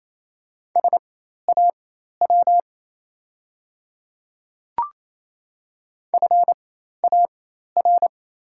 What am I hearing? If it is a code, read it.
Morse code, heard as a keyed tone, about 27 wpm, spaced as for 13 wpm, 690 Hz. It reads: SAW FAR
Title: 4x Character Spacing - Top 500 Words - 28wpm